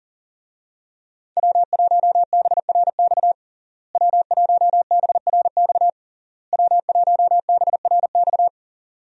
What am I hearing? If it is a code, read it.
Morse code, heard as a keyed tone, about 40 wpm, 700 Hz. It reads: W1BRX W1BRX W1BRX
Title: USA Call Signs - 5 Characters - Repeated 3x - 40wpm